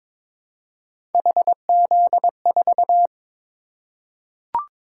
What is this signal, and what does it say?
Morse code, heard as a keyed tone, about 22 wpm, 700 Hz. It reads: HZ4
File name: practice / 22wpm / Three Letter-Number Combinations - 22wpm